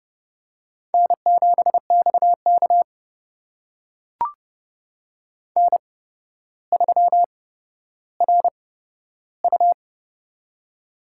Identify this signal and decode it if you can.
Morse code, heard as a keyed tone, about 30 wpm, spaced as for 13 wpm, 700 Hz. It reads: N7XK N3RU
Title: USA Call Signs - 4 Characters - 8x Character Spacing - 30wpm